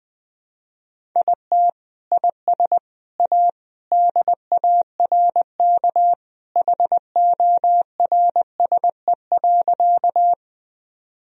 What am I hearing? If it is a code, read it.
Morse code, heard as a keyed tone, about 21 wpm, 700 Hz. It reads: IT IS A DARK HORSE.
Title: Sentences from Top 1000 Words - Review Part 1 - 20wpm